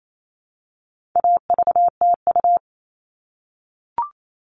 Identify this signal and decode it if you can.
Morse code, heard as a keyed tone, about 28 wpm, 700 Hz. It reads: AVTU
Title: All Cut Numbers - 4-Digit 28wpm